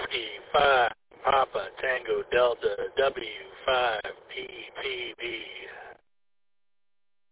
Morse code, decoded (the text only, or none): none